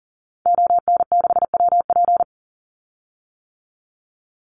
Morse code, heard as a keyed tone, about 40 wpm, 700 Hz. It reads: ON6WP
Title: Call Signs - 40wpm